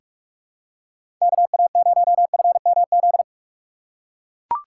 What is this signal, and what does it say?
Morse code, heard as a keyed tone, about 42 wpm, 700 Hz. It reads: KA0FMZ
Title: USA Call Signs - 6 Characters - 3x Character Spacing - 45wpm